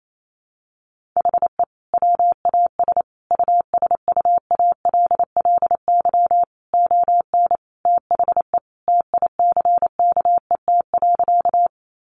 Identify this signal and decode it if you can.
Morse code, heard as a keyed tone, about 28 wpm, 700 Hz. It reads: HE WAS USUALLY ON THE TICKET.
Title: Sentences from Top 1400 Words 28wpm